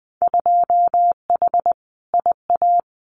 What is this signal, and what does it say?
Morse code, heard as a keyed tone, about 20 wpm, 700 Hz. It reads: 2H IA
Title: Winter Field Day Exchanges Encore 20wpm